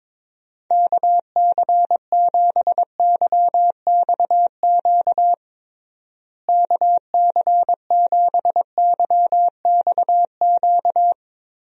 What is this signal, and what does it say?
Morse code, heard as a keyed tone, about 22 wpm, 700 Hz. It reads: KC7YXQ KC7YXQ
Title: USA Call Signs - 6 Characters - Repeated 3x - 22wpm